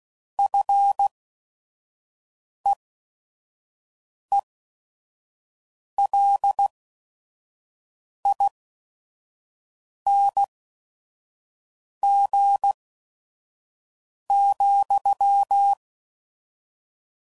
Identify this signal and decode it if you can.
Morse code, heard as a keyed tone, about 16 wpm, spaced as for 5 wpm, 800 Hz. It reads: FEELING,
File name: Mixed / Short Runs / SinglePrefixes